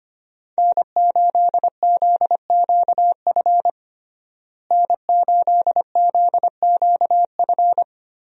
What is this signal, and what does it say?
Morse code, heard as a keyed tone, about 25 wpm, 700 Hz. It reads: N8ZQF N8ZQF
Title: USA Call Signs - 5 Characters - Repeated 3x - 25wpm